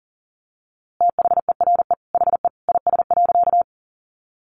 Morse code, heard as a keyed tone, about 40 wpm, 700 Hz. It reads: THERE HE IS.
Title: A Sound Of Thunder 40wpm